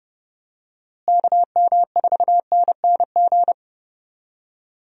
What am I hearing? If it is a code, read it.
Morse code, heard as a keyed tone, about 31 wpm, 700 Hz. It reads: KM4NNG
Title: USA Call Signs - 6 Characters - Repeated 2x - 30wpm